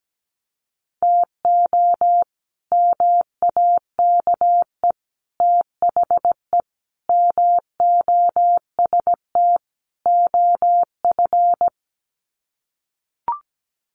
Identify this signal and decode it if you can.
Morse code, heard as a keyed tone, about 17 wpm, 700 Hz. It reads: TO MAKE THE MOST OF
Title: Sets of 5 Words Encore2 17wpm